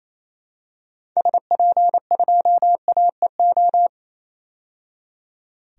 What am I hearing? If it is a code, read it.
Morse code, heard as a keyed tone, about 28 wpm, 700 Hz. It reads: SP2AEO
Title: DXCC-100 Call Signs - 6 Characters - 28wpm